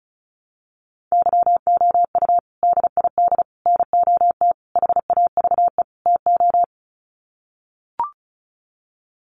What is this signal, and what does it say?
Morse code, heard as a keyed tone, about 34 wpm, 700 Hz. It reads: YOU DID NOT HAVE TO
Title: Sets of 5 Words Encore 35wpm